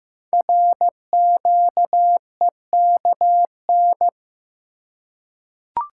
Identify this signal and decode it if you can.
Morse code, heard as a keyed tone, about 15 wpm, 700 Hz. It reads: RQEKN
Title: Code Groups - 5 Letters - 15wpm